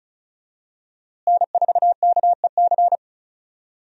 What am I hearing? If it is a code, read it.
Morse code, heard as a keyed tone, about 36 wpm, 700 Hz. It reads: N4KEC
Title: USA Call Signs - 5 Characters - Repeated 3x - 35wpm